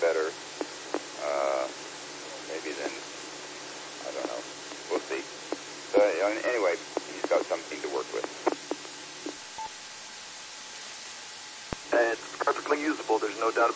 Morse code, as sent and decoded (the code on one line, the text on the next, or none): none